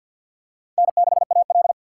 Morse code, heard as a keyed tone, about 52 wpm, 700 Hz. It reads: N6AL
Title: USA Call Signs - 2x Character Spacing - 50wpm